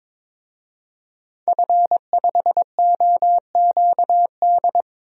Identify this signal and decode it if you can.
Morse code, heard as a keyed tone, about 21 wpm, 700 Hz. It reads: F5OQD
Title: DXCC-100 Call Signs - 5 Characters - 22wpm